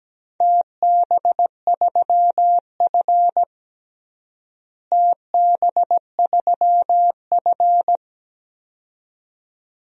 Morse code, heard as a keyed tone, about 17 wpm, 700 Hz. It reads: TB3F TB3F
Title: DXCC-100 Call Signs - 4 Characters - Repeated 2x - 17wpm